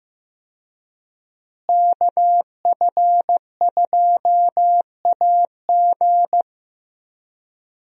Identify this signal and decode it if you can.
Morse code, heard as a keyed tone, about 15 wpm, 700 Hz. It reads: KF2AG
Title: USA Call Signs - 5 Characters - Repeated 3x - 15wpm